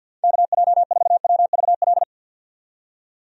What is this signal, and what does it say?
Morse code, heard as a keyed tone, about 52 wpm, 700 Hz. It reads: KJ4WVL
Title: USA Call Signs - 6 Characters - 50wpm